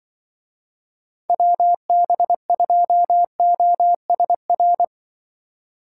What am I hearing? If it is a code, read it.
Morse code, heard as a keyed tone, about 24 wpm, 700 Hz. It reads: WB2OSR
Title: Speed Racing - USA 2x3 Call Signs 20wpm